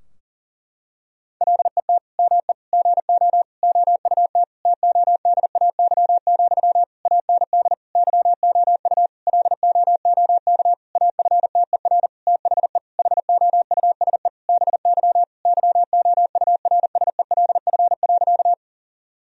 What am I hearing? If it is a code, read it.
Morse code, heard as a keyed tone, about 40 wpm, 700 Hz. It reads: LET ME GO OUT TODAY, AND YOU LOOK AFTER THE HOUSE BY YOURSELF.
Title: Sentences from Top 1000 Words - Review Part 4 - 40wpm